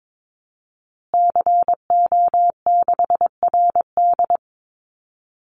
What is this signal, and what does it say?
Morse code, heard as a keyed tone, about 22 wpm, 700 Hz. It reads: CO6RD
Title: Call Signs Repeated 2x - 22wpm